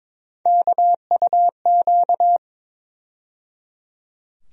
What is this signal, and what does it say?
Morse code, heard as a keyed tone, about 22 wpm, 700 Hz. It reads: KUQ
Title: USA Call Sign Postfixes - 3 Letters 22wpm